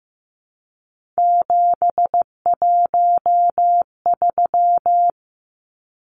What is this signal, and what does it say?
Morse code, heard as a keyed tone, about 15 wpm, 700 Hz. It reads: 713
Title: Numbers - 3-Digit 15wpm